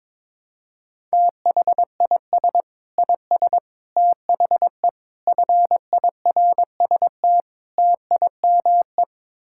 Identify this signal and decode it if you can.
Morse code, heard as a keyed tone, about 22 wpm, 700 Hz. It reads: THIS IS THE FIRST TIME
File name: practice / 22wpm / Build A Phrase - 5-Word Phrases - 22wpm